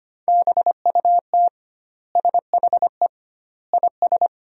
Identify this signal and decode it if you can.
Morse code, heard as a keyed tone, about 25 wpm, 700 Hz. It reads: BUT SHE IS
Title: 2x Word-Spacing - Sentences from Top 400 Words - 25wpm